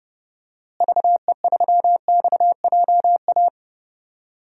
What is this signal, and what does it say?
Morse code, heard as a keyed tone, about 30 wpm, 700 Hz. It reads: VE3XJA
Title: DXCC-100 Call Signs - 6 Characters - Repeated 2x - 30wpm